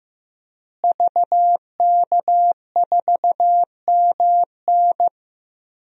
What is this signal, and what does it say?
Morse code, heard as a keyed tone, about 16 wpm, 700 Hz. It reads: VK4MN
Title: DXCC-100 Call Signs - 5 Characters - Repeated 2x - 15wpm